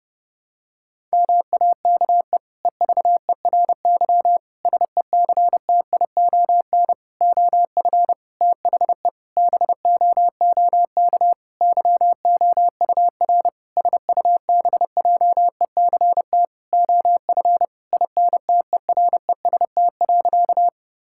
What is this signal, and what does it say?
Morse code, heard as a keyed tone, about 30 wpm, 700 Hz. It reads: MAKE EVERY SECTION OF THE BOOK YOUR SUBJECT OF INTEREST.
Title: Sentences from Top 1000 Words - Review Part 5 - 30wpm